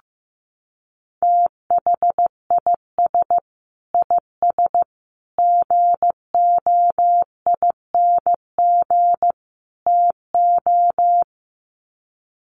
Morse code, heard as a keyed tone, about 15 wpm, 700 Hz. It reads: THIS IS GOING TO
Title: Sets of 4 Words 15wpm